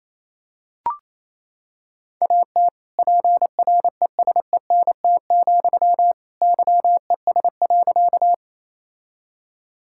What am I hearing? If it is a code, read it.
Morse code, heard as a keyed tone, about 28 wpm, 700 Hz. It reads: AT PRESENT, YES.
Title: Sentences from Top 1000 Words - Review Part 5 - 28wpm